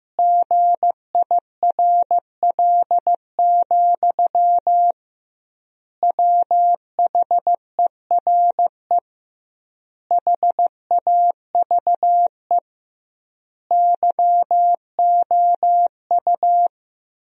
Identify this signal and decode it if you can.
Morse code, heard as a keyed tone, about 15 wpm, 700 Hz. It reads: GIRL, WHERE HAVE YOU
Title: 2x Word-Spacing - Sentences from Top 400 Words - 15wpm